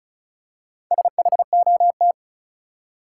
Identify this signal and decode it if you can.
Morse code, heard as a keyed tone, about 34 wpm, 700 Hz. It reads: SHOT
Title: Four Letter Words - 35wpm